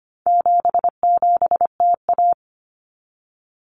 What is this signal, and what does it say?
Morse code, heard as a keyed tone, about 25 wpm, 700 Hz. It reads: 77TA
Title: Common Cut Numbers - 4-Digit 25wpm